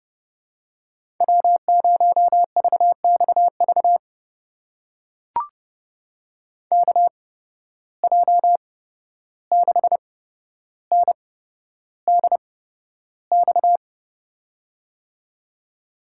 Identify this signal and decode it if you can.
Morse code, heard as a keyed tone, about 30 wpm, spaced as for 12 wpm, 700 Hz. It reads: W0VXV KJ6NDX
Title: USA Call Signs - 8x Character Spacing - 30wpm